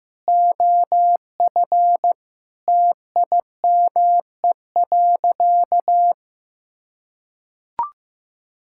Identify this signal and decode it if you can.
Morse code, heard as a keyed tone, about 15 wpm, 700 Hz. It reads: OF TIME.
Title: Sentences from Top 1000 Words - Review Part 5 - 15wpm